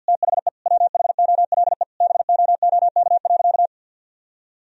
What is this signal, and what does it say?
Morse code, heard as a keyed tone, about 51 wpm, 700 Hz. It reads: THE WHOLE BOOK.
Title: Sentences from Top 1000 Words - Review Part 1 - 50wpm